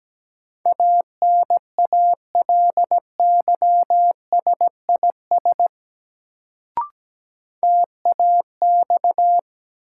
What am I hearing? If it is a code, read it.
Morse code, heard as a keyed tone, about 17 wpm, 700 Hz. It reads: ANALYSIS TAX
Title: Sentences from Top 1000 Words - Review Part 5 - New Word Drill - 17wpm